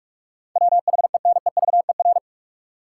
Morse code, heard as a keyed tone, about 43 wpm, 700 Hz. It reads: WHENEVER
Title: Sentences from Top 1200 Words - New Word Drill Alt - 45wpm